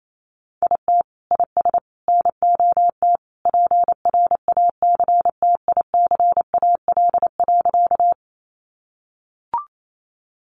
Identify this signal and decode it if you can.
Morse code, heard as a keyed tone, about 28 wpm, 700 Hz. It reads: IT IS NOT PRACTICAL.
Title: Sentences from Top 1400 Words 28wpm